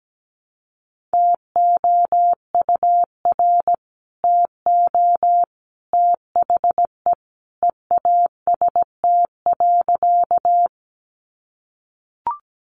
Morse code, heard as a keyed tone, about 17 wpm, 700 Hz. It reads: TOUR TO THE EAST.
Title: Sentences from Top 1400 Words 17wpm